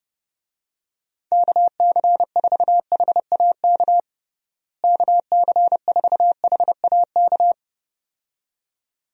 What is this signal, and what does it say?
Morse code, heard as a keyed tone, about 32 wpm, 700 Hz. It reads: KC4HAK KC4HAK
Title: USA Call Signs - 6 Characters - Repeated 2x - 30wpm